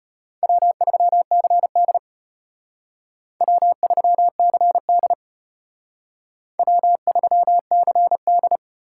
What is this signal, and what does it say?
Morse code, heard as a keyed tone, about 31 wpm, 700 Hz. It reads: W3CD W3CD W3CD
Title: Speed Racing - USA Call Signs 25wpm